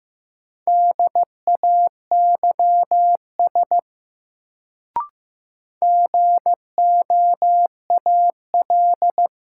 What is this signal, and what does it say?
Morse code, heard as a keyed tone, about 15 wpm, 700 Hz. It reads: DAYS GOAL
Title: Sentences from Top 1000 Words - Review Part 5 - New Word Drill - 15wpm